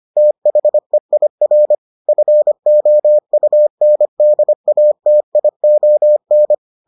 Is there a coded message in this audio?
Yes: Morse code from 0.2 to 6.5 s, heard as a keyed tone, about 25 wpm, 600 Hz. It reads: THEIR FOUNDATION